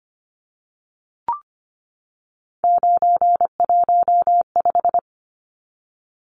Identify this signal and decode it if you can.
Morse code, heard as a keyed tone, about 25 wpm, 700 Hz. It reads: 915
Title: Numbers - 3-Digit 25wpm